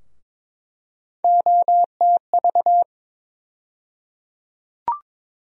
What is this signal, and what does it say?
Morse code, heard as a keyed tone, about 22 wpm, 700 Hz. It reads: OTV
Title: USA Call Sign Postfixes - 3 Letters 22wpm